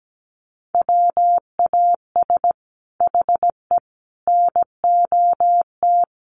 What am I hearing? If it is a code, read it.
Morse code, heard as a keyed tone, about 17 wpm, 700 Hz. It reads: WAS HE NOT
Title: Sentences from Top 1400 Words 17wpm